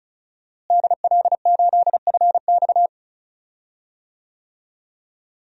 Morse code, heard as a keyed tone, about 36 wpm, 700 Hz. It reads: DL8FX
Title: DXCC-100 Call Signs - 5 Characters - Repeated 3x - 35wpm